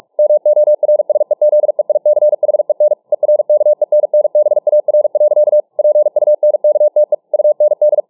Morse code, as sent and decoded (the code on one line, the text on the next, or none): -- --- .-. ... --.. .. -.-. .... -. / . .-. -.- . -. -. -... .- .-. .-.-.- / .--. ..- -. -.- - . / ..- -. -..
MORSEZEICHEN ERKENNBAR. PUNKTE UND